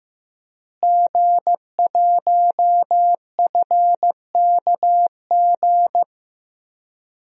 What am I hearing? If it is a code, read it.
Morse code, heard as a keyed tone, about 15 wpm, 700 Hz. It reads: G1FKG
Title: DXCC-100 Call Signs - 5 Characters - Repeated 2x - 15wpm